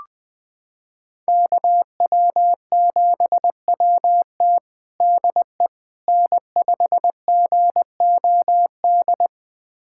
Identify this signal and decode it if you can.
Morse code, heard as a keyed tone, about 20 wpm, 700 Hz. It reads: KW7WT DE N5GOD
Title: USA Callsign DE Callsign - 5 Characters - 20wpm